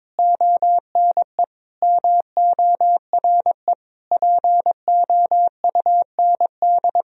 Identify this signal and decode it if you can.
Morse code, heard as a keyed tone, about 21 wpm, 700 Hz. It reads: ONE MORE POUND
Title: Sentences from Top 1000 Words - Review Part 2 - 22wpm